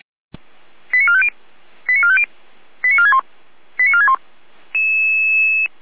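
Five-tone selective call (ZVEI), heard as quick runs of five tones, tones of about 70 ms. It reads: 89450 89450 89461 89461
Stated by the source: ZVEI five-tone call received by SDR.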